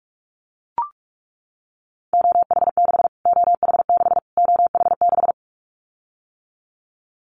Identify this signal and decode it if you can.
Morse code, heard as a keyed tone, about 44 wpm, 700 Hz. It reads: OH6 OH6 OH6
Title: Callsign Prefixes for Top 25 DXCC - Repeated 3x - 45wpm